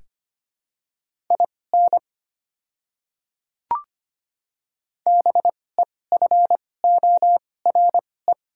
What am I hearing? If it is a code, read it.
Morse code, heard as a keyed tone, about 25 wpm, spaced as for 18 wpm, 700 Hz. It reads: IN BEFORE
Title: 2x Character Spacing - Top 200 Words - 25wpm